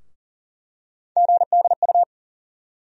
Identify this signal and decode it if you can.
Morse code, heard as a keyed tone, about 40 wpm, 700 Hz. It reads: GDU
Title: USA Call Sign Postfixes - 3 Letters 40wpm